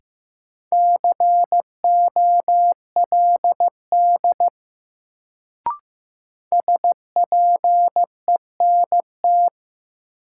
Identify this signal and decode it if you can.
Morse code, heard as a keyed tone, about 15 wpm, 700 Hz. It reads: COLD SPENT
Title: Sets of 5 Words - New Word Drill - 15wpm